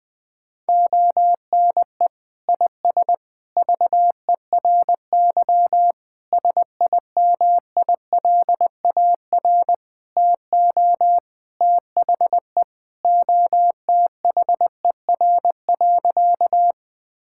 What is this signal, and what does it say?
Morse code, heard as a keyed tone, about 20 wpm, 700 Hz. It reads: ONE IS VERY SIMILAR TO THE OTHER.
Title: Sentences from Top 1000 Words - Review Part 5 - 20wpm